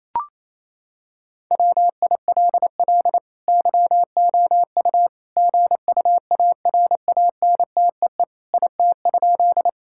Morse code, heard as a keyed tone, about 27 wpm, 700 Hz. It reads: WILL YOU GUARANTEE IT?
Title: Sentences from Top 1600 Words 28wpm